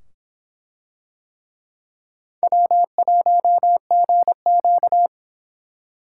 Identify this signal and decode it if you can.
Morse code, heard as a keyed tone, about 24 wpm, 700 Hz. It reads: W1GQ